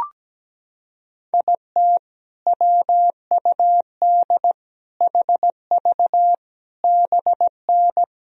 Morse code, heard as a keyed tone, about 17 wpm, 700 Hz. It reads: IT WUD HV BN
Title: Sentences from Top 400 Words and Abbreviations 17wpm